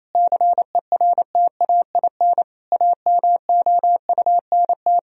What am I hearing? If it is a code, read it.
Morse code, heard as a keyed tone, about 28 wpm, 700 Hz. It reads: CERTAIN AMOUNT
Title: Sentences from Top 1000 Words - Review Part 4 - 28wpm